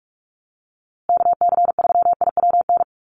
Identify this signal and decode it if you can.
Morse code, heard as a keyed tone, about 44 wpm, 700 Hz. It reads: KC3IWN
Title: USA Call Signs Repeated 2x 45wpm